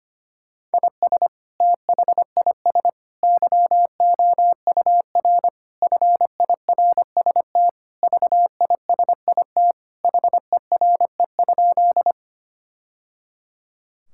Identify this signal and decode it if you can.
Morse code, heard as a keyed tone, about 25 wpm, 700 Hz. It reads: IS THIS YOUR FIRST VISIT HERE?
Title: Sentences from Top 1000 Words - Review Part 6 - 25wpm